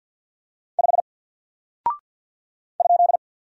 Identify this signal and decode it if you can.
Morse code, heard as a keyed tone, about 50 wpm, 700 Hz. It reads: 5 ?
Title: Single Character (Letter:Number:Punctuation:Prosign) 50wpm